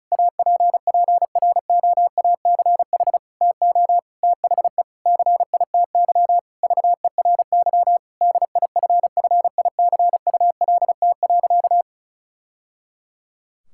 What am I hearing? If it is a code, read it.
Morse code, heard as a keyed tone, about 36 wpm, 700 Hz. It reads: APPROACH TO THE CITY VERY DIFFICULT.
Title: Sentences from Top 1000 Words - Review Part 6 - 35wpm